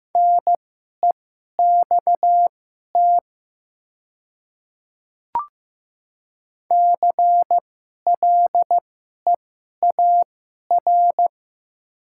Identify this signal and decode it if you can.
Morse code, heard as a keyed tone, about 15 wpm, spaced as for 11 wpm, 700 Hz. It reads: NEXT CLEAR
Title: 2x Character Spacing - Top 500 Words - 15wpm